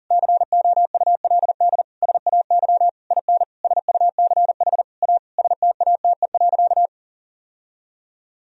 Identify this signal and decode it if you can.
Morse code, heard as a keyed tone, about 40 wpm, 700 Hz. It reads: COULD SAY IN SUCH A STATE.
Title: Sentences from Top 1000 Words - Review Part 2 - 40wpm